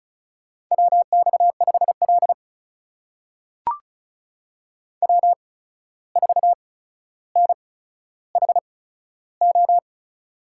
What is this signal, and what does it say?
Morse code, heard as a keyed tone, about 36 wpm, spaced as for 14 wpm, 700 Hz. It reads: WX5L W4NHO
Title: USA Call Signs - 8x Character Spacing - 35wpm